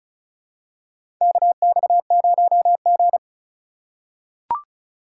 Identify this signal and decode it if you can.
Morse code, heard as a keyed tone, about 34 wpm, 700 Hz. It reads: KX0G
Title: USA Call Signs - 4 Characters - 4x Character Spacing - 35wpm